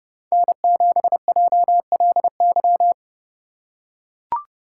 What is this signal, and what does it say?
Morse code, heard as a keyed tone, about 30 wpm, 700 Hz. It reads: N7JLY
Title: USA Call Signs - 5 Characters - 4x Character Spacing - 30wpm